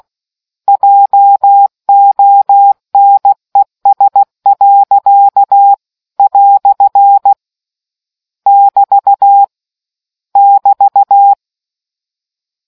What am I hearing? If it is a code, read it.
Morse code, heard as a keyed tone, about 16 wpm, 800 Hz. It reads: JONES. " <BT> <BT>